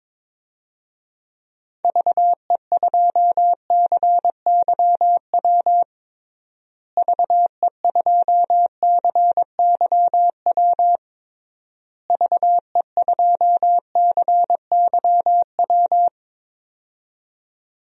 Morse code, heard as a keyed tone, about 22 wpm, 700 Hz. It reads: VE2CYW VE2CYW VE2CYW
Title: DXCC-100 Call Signs - 6 Characters - Repeated 3x - 22wpm